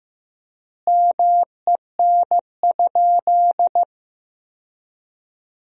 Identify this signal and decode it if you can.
Morse code, heard as a keyed tone, about 15 wpm, 700 Hz. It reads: MEN?
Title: 4x Word-Spacing - Sentences from Top 200 Words - 15wpm